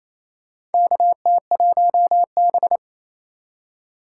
Morse code, heard as a keyed tone, about 29 wpm, 700 Hz. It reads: KT1B
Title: K1USN SST QSO Practice - 28wpm